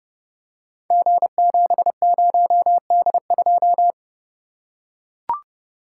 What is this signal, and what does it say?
Morse code, heard as a keyed tone, about 30 wpm, 700 Hz. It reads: G70D2
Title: Code Groups - 5 Letter:Numbers - 30wpm